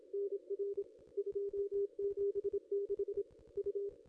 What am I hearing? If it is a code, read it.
Morse code, heard as a keyed tone, about 27 wpm, 440 Hz. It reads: NR 276 U
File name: contests / ss-cw-k5tr-op-n5rz -0500